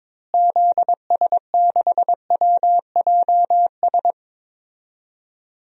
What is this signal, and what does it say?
Morse code, heard as a keyed tone, about 22 wpm, 700 Hz. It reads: ZS6WJS
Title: DXCC-100 Call Signs - 6 Characters - Repeated 3x - 22wpm